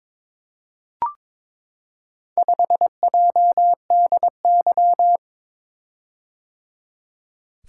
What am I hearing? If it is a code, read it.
Morse code, heard as a keyed tone, about 23 wpm, 700 Hz. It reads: 5JDY
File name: practice / 22wpm / Four Letter-Number Combinations - 22wpm